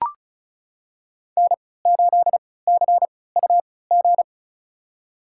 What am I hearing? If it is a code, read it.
Morse code, heard as a keyed tone, about 34 wpm, spaced as for 20 wpm, 700 Hz. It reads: N8CUG
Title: USA Call Signs - 5 Characters - 3x Character Spacing - 35wpm